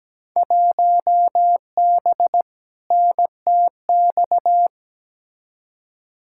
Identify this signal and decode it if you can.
Morse code, heard as a keyed tone, about 17 wpm, 700 Hz. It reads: 1B NTX
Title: Field Day Exchanges - 17wpm